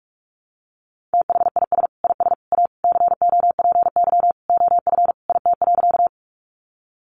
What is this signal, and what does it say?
Morse code, heard as a keyed tone, about 44 wpm, 700 Hz. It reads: THIS IS A COPY OF IT.
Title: Sentences from Top 1400 Words 45wpm